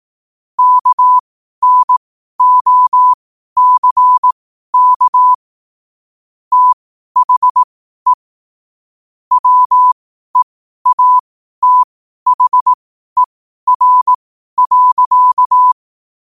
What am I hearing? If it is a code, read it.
Morse code, heard as a keyed tone, about 18 wpm, spaced as for 12 wpm, 1000 Hz. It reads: KNOCK THE WEATHER.
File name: qotdfiles / 12wpm